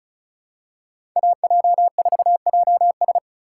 Morse code, heard as a keyed tone, about 36 wpm, 700 Hz. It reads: AJ4JS
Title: USA Call Signs - 5 Characters - 4x Character Spacing - 35wpm